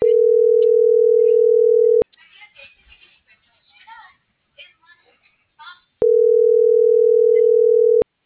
Ringback tone (North America), 440 + 480 Hz.
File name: CENTRAL SAGOT / out edgar.sagot